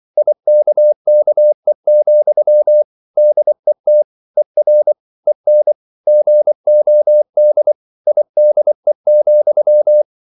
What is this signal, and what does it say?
Morse code, heard as a keyed tone, about 24 wpm, 600 Hz. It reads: IKKE, DET ER EN GOD IDE,
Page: Ferie 24 wpm